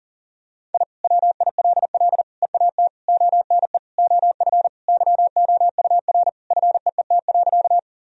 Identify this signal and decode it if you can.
Morse code, heard as a keyed tone, about 40 wpm, 700 Hz. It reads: I WILL EAT ONE OF YOUR FEET.
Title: Sentences from Top 1000 Words - Review Part 1 - 40wpm